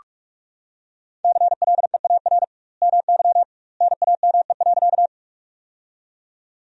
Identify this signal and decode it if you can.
Morse code, heard as a keyed tone, about 48 wpm, 700 Hz. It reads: CLEAR MY NAME.
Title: 2x Word-Spacing - Sentences from Top 500 Words - 45wpm